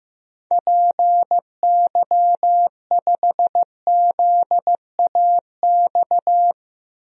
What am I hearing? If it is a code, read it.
Morse code, heard as a keyed tone, about 15 wpm, 700 Hz. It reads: PY5ZAX
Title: DXCC-100 Call Signs - 6 Characters - 15wpm